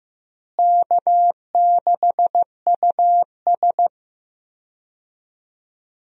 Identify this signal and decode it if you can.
Morse code, heard as a keyed tone, about 15 wpm, 700 Hz. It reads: K6US
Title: USA Call Signs - 4 Characters - Repeated 2x - 15wpm